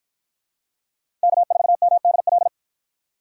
Morse code, heard as a keyed tone, about 50 wpm, 700 Hz. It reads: K4MDL